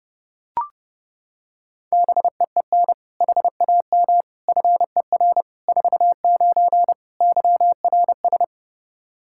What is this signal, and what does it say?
Morse code, heard as a keyed tone, about 29 wpm, 700 Hz. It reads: BEEN HAM FER 49 YRS
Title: K1USN SST QSO Practice 30wpm